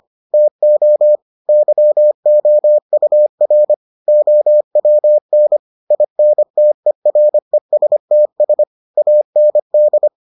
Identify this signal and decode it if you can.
Morse code, heard as a keyed tone, about 25 wpm, 600 Hz. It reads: TO YOUR OWN INTERESTS AND